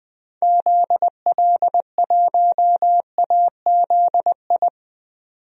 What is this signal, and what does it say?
Morse code, heard as a keyed tone, about 21 wpm, 700 Hz. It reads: ZL1AZI
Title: DXCC-100 Call Signs - 6 Characters - 20wpm